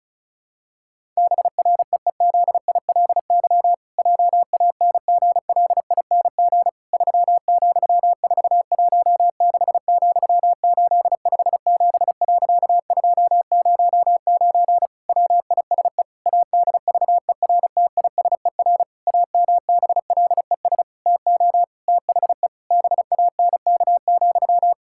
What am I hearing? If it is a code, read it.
Morse code, heard as a keyed tone, about 36 wpm, 700 Hz. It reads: BREEZILY JANGLING 3,416,857.209 WISE ADVERTISER AMBLES TO THE BANK,
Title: Pangram Challenge v2 - 35wpm